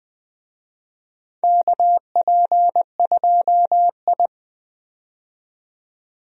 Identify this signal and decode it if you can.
Morse code, heard as a keyed tone, about 20 wpm, 700 Hz. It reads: KP2I
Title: DXCC-100 Call Signs - 4 Characters - 20wpm